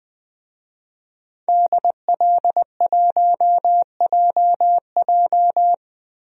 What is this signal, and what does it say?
Morse code, heard as a keyed tone, about 20 wpm, 700 Hz. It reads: DL1JJ
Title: DXCC-100 Call Signs - 5 Characters - 20wpm